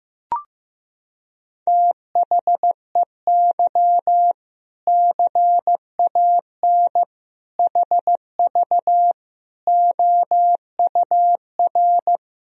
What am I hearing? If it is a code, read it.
Morse code, heard as a keyed tone, about 15 wpm, 700 Hz. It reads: THEY CAN HV OUR
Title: Sentences from Top 400 Words and Abbreviations 15wpm